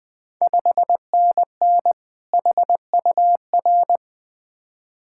Morse code, heard as a keyed tone, about 20 wpm, 700 Hz. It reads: 5NN HUR
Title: Ontario QSO Party Exchange 20wpm